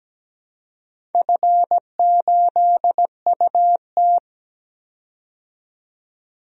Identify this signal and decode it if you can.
Morse code, heard as a keyed tone, about 17 wpm, 700 Hz. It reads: F8UT
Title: DXCC-100 Call Signs - 4 Characters - 17wpm